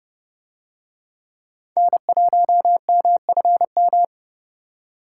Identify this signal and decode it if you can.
Morse code, heard as a keyed tone, about 28 wpm, 700 Hz. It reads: N1MFM